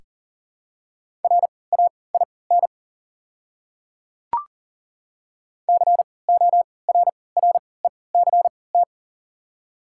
Morse code, heard as a keyed tone, about 40 wpm, spaced as for 23 wpm, 700 Hz. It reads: RAIN CORRECT
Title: 3x Character Spacing - Top 500 Words - 40wpm